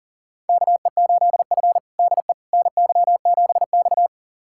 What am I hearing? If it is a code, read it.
Morse code, heard as a keyed tone, about 37 wpm, 700 Hz. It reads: KE8F DE NY7X
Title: USA Callsign DE Callsign - 4 Characters - 40wpm